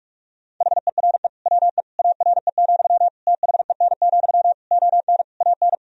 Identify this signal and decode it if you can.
Morse code, heard as a keyed tone, about 46 wpm, 700 Hz. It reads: HERE WE ARE, THEN, ON AN
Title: Sentences from Top 1000 Words - Review Part 3 - 45wpm